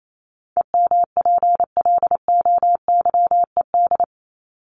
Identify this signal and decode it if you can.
Morse code, heard as a keyed tone, about 28 wpm, 700 Hz. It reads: EMPLOYED
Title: ED Suffix - Top 100 Words with Base Words 28wpm